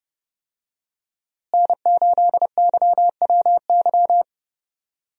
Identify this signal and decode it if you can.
Morse code, heard as a keyed tone, about 32 wpm, 700 Hz. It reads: N8YWY
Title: USA Call Signs - 5 Characters - Repeated 3x - 30wpm